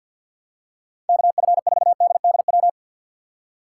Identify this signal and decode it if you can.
Morse code, heard as a keyed tone, about 46 wpm, 700 Hz. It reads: KF4DDW
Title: USA Call Signs - 6 Characters - 4x Character Spacing - 50wpm